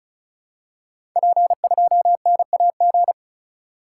Morse code, heard as a keyed tone, about 36 wpm, 700 Hz. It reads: P2NAG
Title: Code Groups - 5 Letter:Numbers - 35wpm